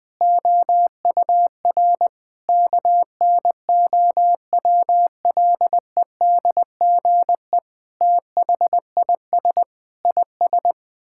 Morse code, heard as a keyed tone, about 21 wpm, 700 Hz. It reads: OUR KNOWLEDGE THIS IS
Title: Sets of 8 Words - 20wpm